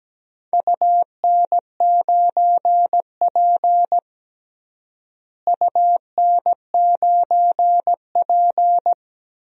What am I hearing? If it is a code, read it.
Morse code, heard as a keyed tone, about 17 wpm, 700 Hz. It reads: UN9P UN9P
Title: DXCC-100 Call Signs - 4 Characters - Repeated 2x - 17wpm